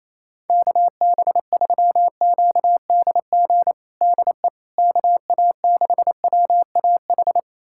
Morse code, heard as a keyed tone, about 28 wpm, 700 Hz. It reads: KB3QDG DE KA6WAH
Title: USA Callsign DE Callsign - 6 Characters - 28wpm